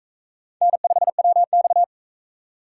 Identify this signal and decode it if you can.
Morse code, heard as a keyed tone, about 44 wpm, 700 Hz. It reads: N5WX